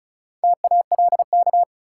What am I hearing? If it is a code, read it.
Morse code, heard as a keyed tone, about 34 wpm, 700 Hz. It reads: TALK
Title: Sentences from Top 400 Words - New Word Drill - 35wpm